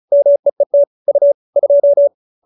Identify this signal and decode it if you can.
Morse code, heard as a keyed tone, about 34 wpm, 590 Hz. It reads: MEET U 2